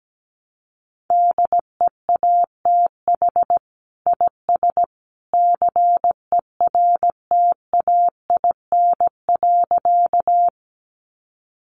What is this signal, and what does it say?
Morse code, heard as a keyed tone, about 17 wpm, 700 Hz. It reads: DEATH IS CERTAIN.
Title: Sentences from Top 1100 Words 17wpm